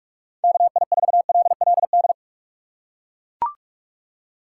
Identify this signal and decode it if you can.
Morse code, heard as a keyed tone, about 41 wpm, 700 Hz. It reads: KI4LLD
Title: USA Call Signs - 6 Characters - 2x Character Spacing - 45wpm